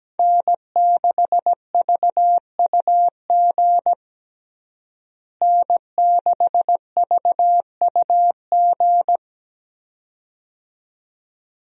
Morse code, heard as a keyed tone, about 17 wpm, 700 Hz. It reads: N6VUG N6VUG
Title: USA Call Signs - 5 Characters - Repeated 2x - 17wpm